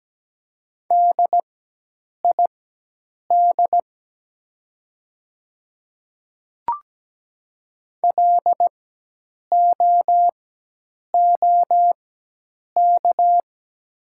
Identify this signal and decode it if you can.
Morse code, heard as a keyed tone, about 17 wpm, spaced as for 8 wpm, 700 Hz. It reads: DID LOOK
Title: 4x Character Spacing - Top 500 Words - 17wpm